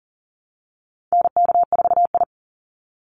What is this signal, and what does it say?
Morse code, heard as a keyed tone, about 37 wpm, 700 Hz. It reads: NK4I
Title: Call Signs Repeated 3x - 40wpm